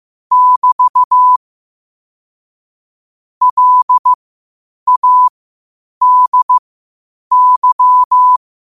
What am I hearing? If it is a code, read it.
Morse code, heard as a keyed tone, about 15 wpm, spaced as for 8 wpm, 1000 Hz. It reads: <BT> LADY
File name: qotdfiles / 8wpm